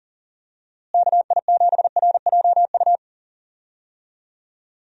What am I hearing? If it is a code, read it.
Morse code, heard as a keyed tone, about 40 wpm, 700 Hz. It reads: KI7RJU
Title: USA Call Signs - 6 Characters - Repeated 3x - 40wpm